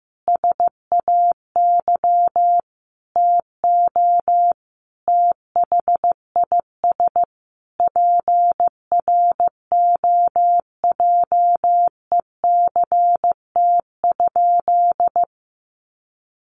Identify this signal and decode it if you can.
Morse code, heard as a keyed tone, about 15 wpm, 700 Hz. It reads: SAY TO THIS PROJECT?
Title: Sentences from Top 1400 Words 15wpm